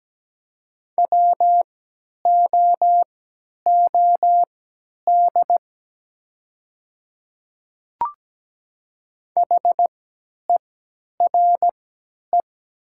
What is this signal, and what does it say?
Morse code, heard as a keyed tone, about 17 wpm, spaced as for 10 wpm, 700 Hz. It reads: WOOD HERE
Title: 3x Character Spacing - Top 500 Words - 17wpm